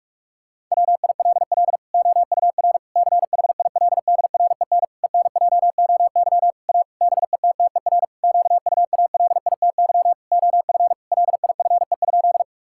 Morse code, heard as a keyed tone, about 46 wpm, 700 Hz. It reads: WILL OUR CHILDREN ENJOY A BETTER QUALITY OF LIFE?
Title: Sentences from Top 1000 Words - Review Part 5 - 45wpm